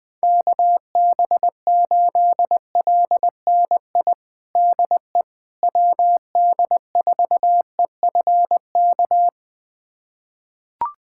Morse code, heard as a keyed tone, about 20 wpm, 700 Hz. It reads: KB8LNI DE WD4EFK
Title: USA Callsign DE Callsign 20wpm